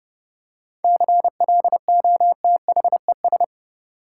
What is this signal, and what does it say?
Morse code, heard as a keyed tone, about 30 wpm, 700 Hz. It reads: CLOTHES
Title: Seven Letter Words - 30wpm